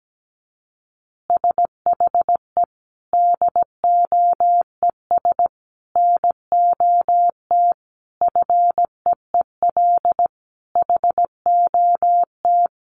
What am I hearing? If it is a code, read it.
Morse code, heard as a keyed tone, about 17 wpm, 700 Hz. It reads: SHE DOES NOT FEEL HOT
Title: Easy Phrases 17wpm